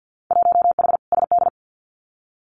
Morse code, heard as a keyed tone, about 50 wpm, 700 Hz. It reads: 1H SD
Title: Winter Field Day Exchanges 50wpm